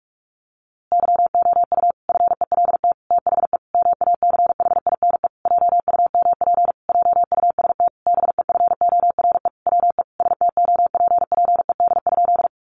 Morse code, heard as a keyed tone, about 44 wpm, 700 Hz. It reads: YOU FELT THE MACHINE JUMP JUST BEFORE WE STOPPED?
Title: A Sound Of Thunder 45wpm